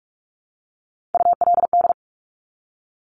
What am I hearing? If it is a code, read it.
Morse code, heard as a keyed tone, about 46 wpm, 700 Hz. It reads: ULD
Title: Top Three Letter Combinations in Words 45wpm